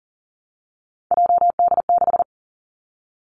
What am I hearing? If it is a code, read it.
Morse code, heard as a keyed tone, about 38 wpm, 700 Hz. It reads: JD6
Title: Three Challenging Letter-Number Combinations 40wpm